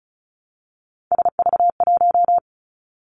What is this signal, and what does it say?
Morse code, heard as a keyed tone, about 36 wpm, 700 Hz. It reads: SV1
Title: Callsign Prefixes for Top 50 DXCC - 35wpm